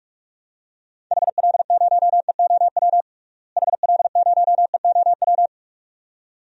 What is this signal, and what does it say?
Morse code, heard as a keyed tone, about 46 wpm, 700 Hz. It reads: HL0EOW HL0EOW
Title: DXCC-100 Call Signs - 6 Characters - Repeated 2x - 45wpm